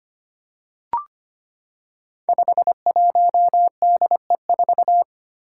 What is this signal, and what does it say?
Morse code, heard as a keyed tone, about 25 wpm, 700 Hz. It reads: 51DE4
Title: Five Challenging Letter-Number Combinations 25wpm